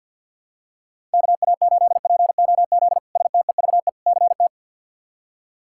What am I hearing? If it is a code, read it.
Morse code, heard as a keyed tone, about 53 wpm, 700 Hz. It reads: KA8POZ STEVE CT
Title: K1USN SST QSO Practice - 50wpm